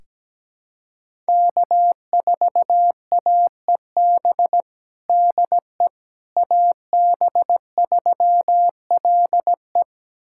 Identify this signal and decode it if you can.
Morse code, heard as a keyed tone, about 17 wpm, 700 Hz. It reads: K4AEB DE AB3LE
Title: USA Callsign DE Callsign - 5 Characters - 17wpm